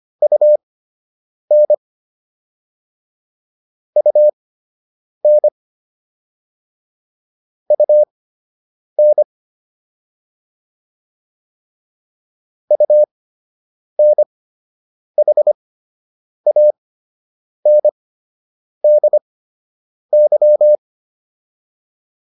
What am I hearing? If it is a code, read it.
Morse code, heard as a keyed tone, about 25 wpm, spaced as for 8 wpm, 600 Hz. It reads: UN UN UN UNHANDY